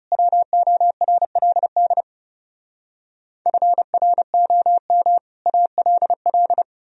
Morse code, heard as a keyed tone, about 37 wpm, 700 Hz. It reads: WORLD FROM ALL
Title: Speed-Racing - Sets of 5 Words 30wpm